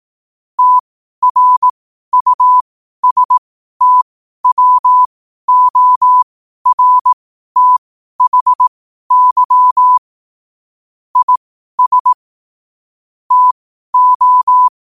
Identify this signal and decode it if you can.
Morse code, heard as a keyed tone, about 18 wpm, spaced as for 12 wpm, 1000 Hz. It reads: TRUSTWORTHY IS TO
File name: qotdfiles / 12wpm